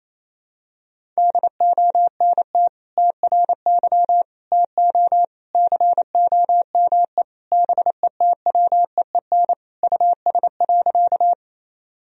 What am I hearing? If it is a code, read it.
Morse code, heard as a keyed tone, about 28 wpm, 700 Hz. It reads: DONT TRY TO COME BETWEEN US.
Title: Sentences from Top 1000 Words - Review Part 1 - 28wpm